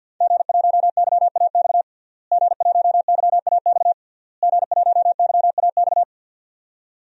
Morse code, heard as a keyed tone, about 52 wpm, 700 Hz. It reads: G1YAX G1YAX G1YAX
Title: DXCC-100 Call Signs - 5 Characters - Repeated 3x - 50wpm